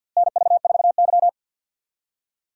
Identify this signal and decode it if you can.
Morse code, heard as a keyed tone, about 48 wpm, 700 Hz. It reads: NV4Y
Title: USA Call Signs - 4 Characters - Repeated 3x - 50wpm